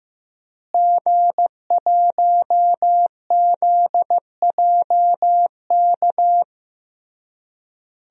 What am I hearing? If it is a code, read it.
Morse code, heard as a keyed tone, about 15 wpm, 700 Hz. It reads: G1ZJK
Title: DXCC-100 Call Signs - 5 Characters - Repeated 3x - 15wpm